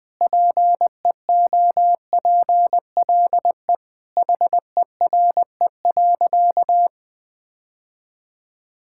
Morse code, heard as a keyed tone, about 20 wpm, 700 Hz. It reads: PEOPLE HERE.
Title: Sentences with 6-Word Phrases - Spoken Beginning - 20wpm